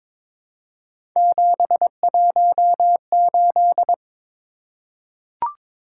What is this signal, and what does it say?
Morse code, heard as a keyed tone, about 22 wpm, 700 Hz. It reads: 718
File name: practice / 22wpm / Three Letter-Number Combinations - 22wpm